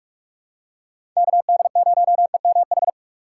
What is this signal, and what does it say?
Morse code, heard as a keyed tone, about 46 wpm, 700 Hz. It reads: KD0EMH
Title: USA Call Signs - 6 Characters - 2x Character Spacing - 45wpm